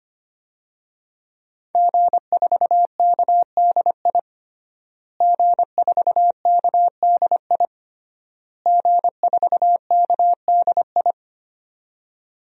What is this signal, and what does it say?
Morse code, heard as a keyed tone, about 25 wpm, 700 Hz. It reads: G4KDI G4KDI G4KDI
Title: DXCC-100 Call Signs - 5 Characters - Repeated 3x - 25wpm